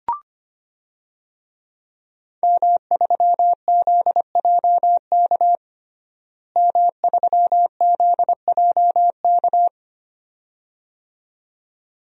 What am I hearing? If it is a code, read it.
Morse code, heard as a keyed tone, about 25 wpm, 700 Hz. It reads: M3ZJK M3ZJK
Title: DXCC-100 Call Signs - 5 Characters - Repeated 2x - 25wpm